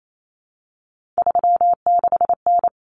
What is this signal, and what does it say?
Morse code, heard as a keyed tone, about 28 wpm, 700 Hz. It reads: 36N
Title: Common Cut Numbers - 3-Digit 28wpm